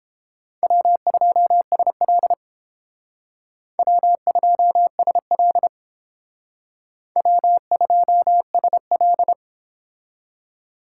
Morse code, heard as a keyed tone, about 32 wpm, 700 Hz. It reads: W2SL W2SL W2SL